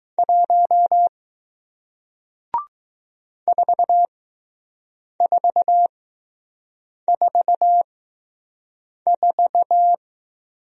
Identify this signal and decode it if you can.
Morse code, heard as a keyed tone, about 20 wpm, spaced as for 6 wpm, 700 Hz. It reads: 1 4444